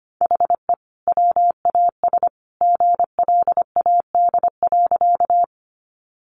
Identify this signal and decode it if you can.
Morse code, heard as a keyed tone, about 25 wpm, 700 Hz. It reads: HE WAS GLAD.
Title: Sentences from Top 800 Words 25wpm